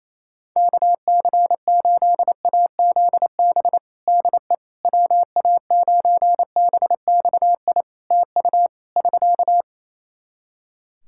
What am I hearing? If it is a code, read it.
Morse code, heard as a keyed tone, about 28 wpm, 700 Hz. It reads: KC8AZB DE WA9BXI TU <SK>
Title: K1USN SST QSO Practice 28wpm